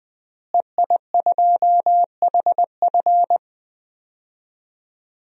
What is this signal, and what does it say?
Morse code, heard as a keyed tone, about 20 wpm, 700 Hz. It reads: EI2HF
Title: DXCC-100 Call Signs - 5 Characters - Repeated 2x - 20wpm